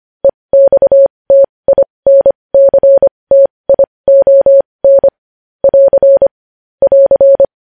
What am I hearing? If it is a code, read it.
Morse code, heard as a keyed tone, about 25 wpm, spaced as for 20 wpm, 550 Hz. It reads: EXTINCTION <AR> <AR>
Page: News Headlines in Morse Code at 20 WPM